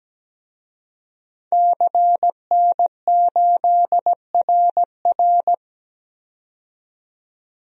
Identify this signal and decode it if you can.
Morse code, heard as a keyed tone, about 17 wpm, 700 Hz. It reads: CN8RR
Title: DXCC-100 Call Signs - 5 Characters - 17wpm